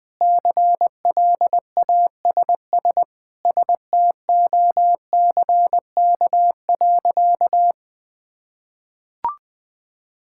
Morse code, heard as a keyed tone, about 20 wpm, 700 Hz. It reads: CLASS STOCK.
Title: Sentences from Top 1000 Words - Review Part 5 - 20wpm